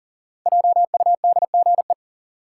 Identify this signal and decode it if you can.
Morse code, heard as a keyed tone, about 40 wpm, 700 Hz. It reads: JUDGE
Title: Five Letter Words - 40wpm